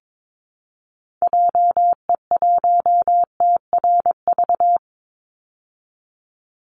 Morse code, heard as a keyed tone, about 22 wpm, 700 Hz. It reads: JE1TRV
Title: Call Signs Repeated 2x - 22wpm